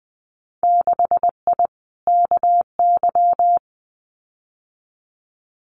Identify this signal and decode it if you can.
Morse code, heard as a keyed tone, about 20 wpm, 700 Hz. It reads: 6I KY
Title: Winter Field Day Exchanges 20wpm